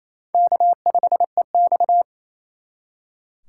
Morse code, heard as a keyed tone, about 34 wpm, 700 Hz. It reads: K5EX
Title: Speed Racing - USA 1x2 Call Signs 28wpm